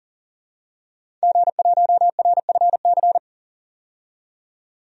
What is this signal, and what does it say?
Morse code, heard as a keyed tone, about 37 wpm, 700 Hz. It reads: G1RFC
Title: DXCC-100 Call Signs - 5 Characters - 40wpm